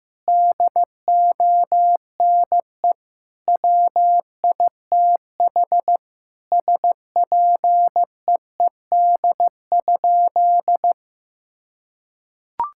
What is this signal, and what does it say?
Morse code, heard as a keyed tone, about 15 wpm, 700 Hz. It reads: DONE WITH SPEED?
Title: Sentences from Top 1000 Words - Review Part 5 - 15wpm